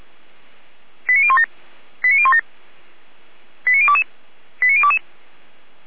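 Five-tone selective call (ZVEI), heard as quick runs of five tones, tones of about 70 ms. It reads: 89017 89017 89022 89022
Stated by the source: ZVEI five-tone selective call received on an SDR.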